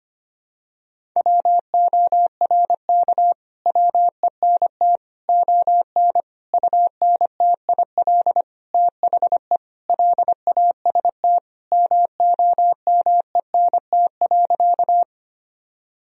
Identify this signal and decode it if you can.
Morse code, heard as a keyed tone, about 25 wpm, 700 Hz. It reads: WORK WENT ON UNTIL THE LAST MOMENT.
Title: Sentences from Top 1000 Words - Review Part 1 - 25wpm